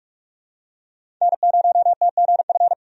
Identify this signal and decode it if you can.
Morse code, heard as a keyed tone, about 46 wpm, 700 Hz. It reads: N0TGF
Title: USA Call Signs - 5 Characters - 45wpm